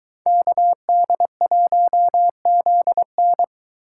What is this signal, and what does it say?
Morse code, heard as a keyed tone, about 23 wpm, 700 Hz. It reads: KD1ZN